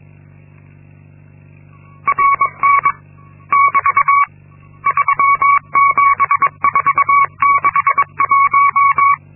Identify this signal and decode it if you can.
Morse code, heard as a keyed tone, about 22 wpm, 1120 Hz. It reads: RN <BT> 37461
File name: Linksmakalnis-2